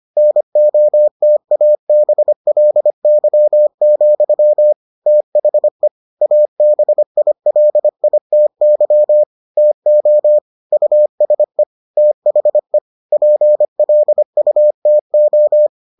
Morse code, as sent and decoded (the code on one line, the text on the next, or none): -. --- - .- -... .-.. -.-- --..-- / - .... / .- -... .. .-.. .. - -.-- / - --- / ..- ... / - .... / .--. .-.. ..- - ---
NOTABLY, THE ABILITY TO USE THE PLUTO